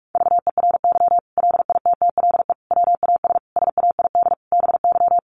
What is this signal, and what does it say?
Morse code, heard as a keyed tone, about 44 wpm, 700 Hz. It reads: VERY LITTLE WAS SAID BY
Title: Sentences from Top 600 Words 45wpm